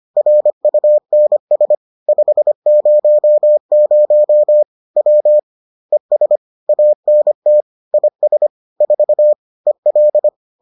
Morse code, heard as a keyed tone, about 25 wpm, 600 Hz. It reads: RUNS 500 W ES ANT IS 4 EL